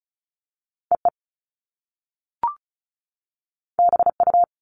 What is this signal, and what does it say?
Morse code, heard as a keyed tone, about 36 wpm, 700 Hz. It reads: EE BU
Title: Top Two Letter Combinations in Words 35wpm